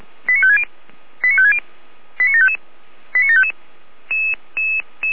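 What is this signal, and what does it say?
Five-tone selective call (ZVEI), heard as quick runs of five tones, tones of about 70 ms. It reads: 89560 89560 89755 89755
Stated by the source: Five-tone ZVEI selective call received on SDR.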